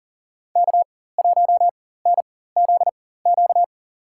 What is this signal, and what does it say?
Morse code, heard as a keyed tone, about 38 wpm, spaced as for 18 wpm, 700 Hz. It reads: K1NZQ
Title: USA Call Signs - 5 Characters - 4x Character Spacing - 40wpm